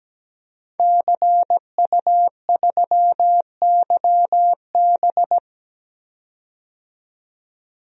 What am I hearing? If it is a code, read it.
Morse code, heard as a keyed tone, about 17 wpm, 700 Hz. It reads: CU3YB
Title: DXCC-100 Call Signs - 5 Characters - 17wpm